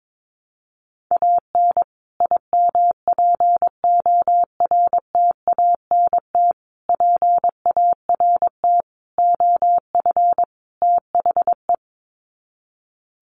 Morse code, heard as a keyed tone, about 22 wpm, 700 Hz. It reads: AN IMPORTANT PART OF THE
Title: Sets of 5 Words Encore 22wpm